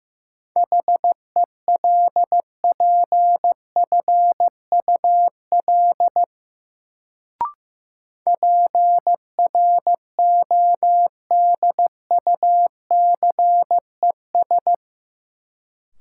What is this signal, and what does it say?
Morse code, heard as a keyed tone, about 15 wpm, 700 Hz. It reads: HELPFUL PRODUCES
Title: Sentences from Top 1000 Words - Review Part 6 - New Word Drill - 15wpm